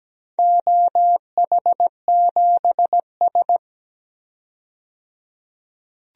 Morse code, heard as a keyed tone, about 17 wpm, 700 Hz. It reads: OH7S
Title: DXCC-100 Call Signs - 4 Characters - 17wpm